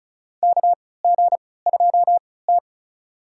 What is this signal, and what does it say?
Morse code, heard as a keyed tone, about 36 wpm, spaced as for 20 wpm, 700 Hz. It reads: KG2T
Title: USA Call Signs - 4 Characters - 3x Character Spacing - 35wpm